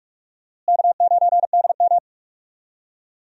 Morse code, heard as a keyed tone, about 46 wpm, 700 Hz. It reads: K9DM
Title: USA Call Signs - 4 Characters - Repeated 3x - 45wpm